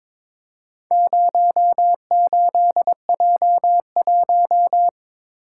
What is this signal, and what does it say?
Morse code, heard as a keyed tone, about 22 wpm, 700 Hz. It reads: 08J1
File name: practice / 22wpm / Four Letter-Number Combinations - 22wpm